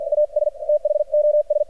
Morse code, heard as a keyed tone, about 53 wpm, spaced as for 45 wpm, 600 Hz, with radio noise. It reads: AS THOS
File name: archived / CRSnet